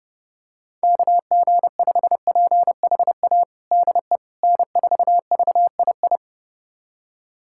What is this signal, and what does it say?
Morse code, heard as a keyed tone, about 30 wpm, 700 Hz. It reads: KG5PHA DE N4VII
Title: USA Callsign DE Callsign 30wpm